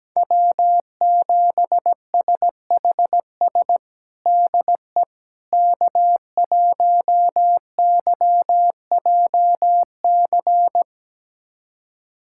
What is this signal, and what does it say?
Morse code, heard as a keyed tone, about 17 wpm, 700 Hz. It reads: W7SHS DE K1YJC
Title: USA Callsign DE Callsign - 5 Characters - 17wpm